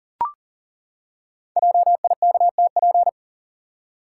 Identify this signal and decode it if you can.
Morse code, heard as a keyed tone, about 42 wpm, 700 Hz. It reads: JIKTP
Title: Code Groups - 5 Letters - 40wpm